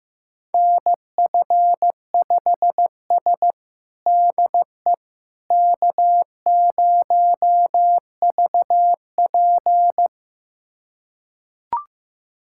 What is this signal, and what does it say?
Morse code, heard as a keyed tone, about 15 wpm, 700 Hz. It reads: NF5S DE K0VP
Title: USA Callsign DE Callsign - 4 Characters - 15wpm